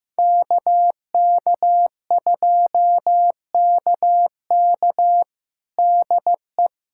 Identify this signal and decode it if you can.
Morse code, heard as a keyed tone, about 15 wpm, 700 Hz. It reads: KK2KK DE
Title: USA Callsign DE Callsign - 5 Characters - 15wpm